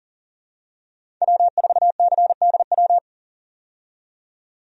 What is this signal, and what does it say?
Morse code, heard as a keyed tone, about 40 wpm, 700 Hz. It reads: W4CDW
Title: USA Call Signs - 5 Characters - 40wpm